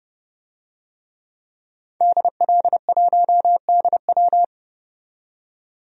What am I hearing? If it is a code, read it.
Morse code, heard as a keyed tone, about 32 wpm, 700 Hz. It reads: DL1DW
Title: DXCC-100 Call Signs - 5 Characters - 30wpm